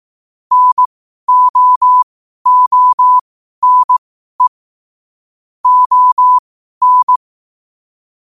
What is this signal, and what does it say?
Morse code, heard as a keyed tone, about 18 wpm, spaced as for 12 wpm, 1000 Hz. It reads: NO ONE ON
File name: qotdfiles / 12wpm